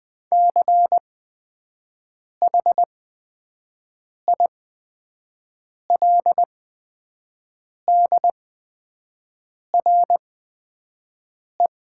Morse code, heard as a keyed tone, about 20 wpm, spaced as for 5 wpm, 700 Hz. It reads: CHILDRE
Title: 8x Character Spacing - Top 300 Words - 20wpm